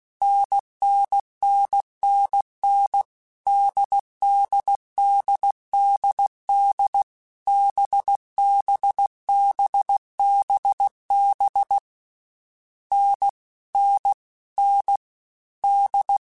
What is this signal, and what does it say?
Morse code, heard as a keyed tone, about 16 wpm, 800 Hz. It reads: NNNNN DDDDD BBBBB N N N D
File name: Short Runs / Short Runs-mp3s / NDB NDB6 @15